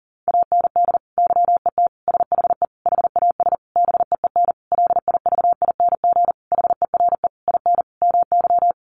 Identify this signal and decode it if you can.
Morse code, heard as a keyed tone, about 40 wpm, 700 Hz. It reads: AND YET SHE HAS BEEN LIVING HERE IN MY
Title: Sentences from Top 1100 Words 40wpm